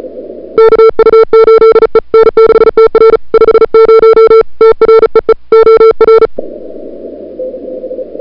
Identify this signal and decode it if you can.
Morse code, heard as a keyed tone, about 35 wpm, 440 Hz. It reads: KU8E N6TR 50 TREE OR
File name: n6tr / Fab2007CWSprintHour1